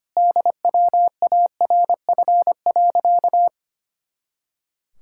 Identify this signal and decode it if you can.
Morse code, heard as a keyed tone, about 25 wpm, 700 Hz. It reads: DWARF.
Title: Pangram Challenge v2 - 25wpm